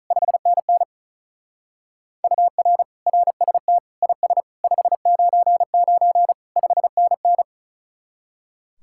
Morse code, heard as a keyed tone, about 46 wpm, 700 Hz. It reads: 5NN UR RST IS 599 5NN